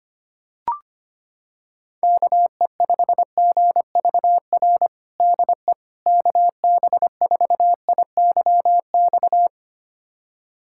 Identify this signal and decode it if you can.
Morse code, heard as a keyed tone, about 25 wpm, 700 Hz. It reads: KE5GVR DE KB4IYX
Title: USA Callsign DE Callsign 25wpm